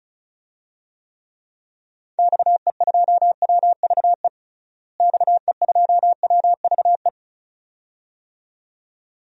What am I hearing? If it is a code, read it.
Morse code, heard as a keyed tone, about 34 wpm, 700 Hz. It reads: XE2WVE XE2WVE
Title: DXCC-100 Call Signs - 6 Characters - Repeated 2x - 35wpm